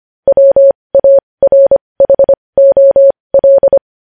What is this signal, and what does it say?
Morse code, heard as a keyed tone, about 25 wpm, spaced as for 20 wpm, 550 Hz. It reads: WARHOL
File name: podcast / News headlines-1x-25x20